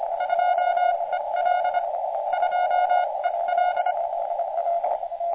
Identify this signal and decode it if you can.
Morse code, heard as a keyed tone, about 25 wpm, spaced as for 22 wpm, 1440 Hz, with radio noise. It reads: 2EL 2EL
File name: JA7BXS- -on-80